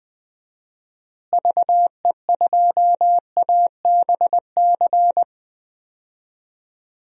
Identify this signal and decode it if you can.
Morse code, heard as a keyed tone, about 21 wpm, 700 Hz. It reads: VE2ABC
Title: DXCC-100 Call Signs - 6 Characters - Repeated 3x - 20wpm